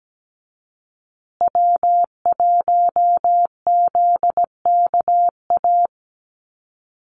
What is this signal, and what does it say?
Morse code, heard as a keyed tone, about 17 wpm, 700 Hz. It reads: W1ZKA
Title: USA Call Signs Repeated 3x 17wpm